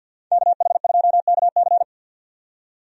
Morse code, heard as a keyed tone, about 52 wpm, 700 Hz. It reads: KH2KC
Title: USA Call Signs - 5 Characters - 50wpm